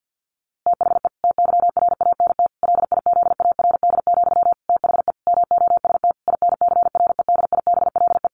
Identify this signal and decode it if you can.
Morse code, heard as a keyed tone, about 50 wpm, 700 Hz. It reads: THE TYRANT LIZARD, THE MOST INCREDIBLE
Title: A Sound Of Thunder 50wpm